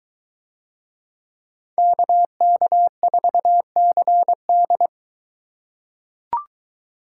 Morse code, heard as a keyed tone, about 22 wpm, 700 Hz. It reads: KK4CD